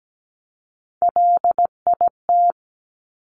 Morse code, heard as a keyed tone, about 17 wpm, 700 Hz. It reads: LIT
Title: Top Three Letter Combinations in Words 17wpm